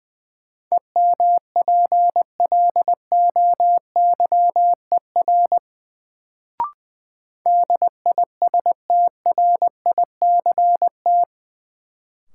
Morse code, heard as a keyed tone, about 20 wpm, 700 Hz. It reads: EMPLOYER DISTRICT
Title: Sentences from Top 1600 Words - New Word Drill - 20wpm